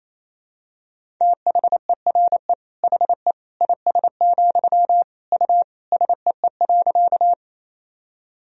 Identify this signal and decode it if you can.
Morse code, heard as a keyed tone, about 28 wpm, 700 Hz. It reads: THERE HE IS, U SEE.
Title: Sentences from Top 100 Words and Abbreviations 28wpm